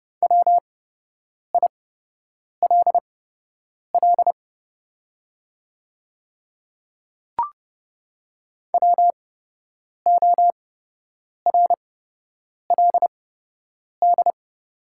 Morse code, heard as a keyed tone, about 30 wpm, spaced as for 8 wpm, 700 Hz. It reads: WILL WORLD
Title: 8x Character Spacing - Top 200 Words - 30wpm